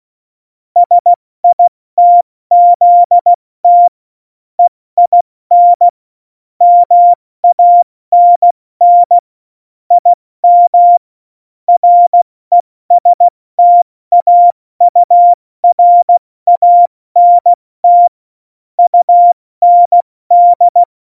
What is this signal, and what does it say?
Morse code, heard as a keyed tone, about 16 wpm, 700 Hz. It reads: SITZT EIN MANN IM RESTAURANT UND
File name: Morsekurs / Witze / Witz 26. November 2019, 70 F80